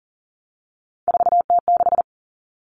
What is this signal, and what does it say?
Morse code, heard as a keyed tone, about 40 wpm, 700 Hz. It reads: 4T6
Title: Common Cut Numbers - 3-Digit 40wpm